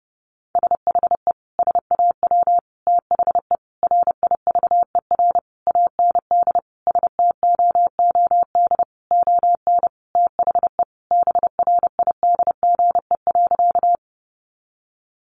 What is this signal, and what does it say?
Morse code, heard as a keyed tone, about 30 wpm, 700 Hz. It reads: SHE SAW THE RIVER AND STOOD ON THE BRIDGE.
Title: Sentences from Top 1200 Words 30wpm